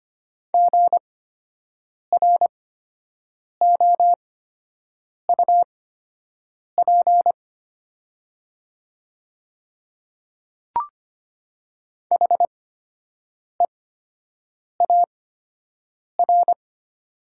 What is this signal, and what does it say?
Morse code, heard as a keyed tone, about 25 wpm, spaced as for 7 wpm, 700 Hz. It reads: GROUP HEAR
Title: 8x Character Spacing - Top 300 Words - 25wpm